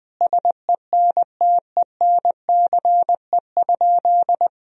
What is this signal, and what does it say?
Morse code, heard as a keyed tone, about 19 wpm, 700 Hz. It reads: SENTENCE?
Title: Sentences from Top 1000 Words - Review Part 1 - 20wpm